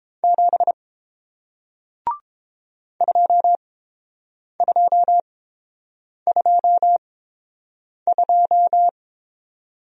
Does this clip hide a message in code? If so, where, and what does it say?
Yes: Morse code from 0.2 to 8.9 s, heard as a keyed tone, about 30 wpm, spaced as for 7 wpm, 700 Hz. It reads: 7 2222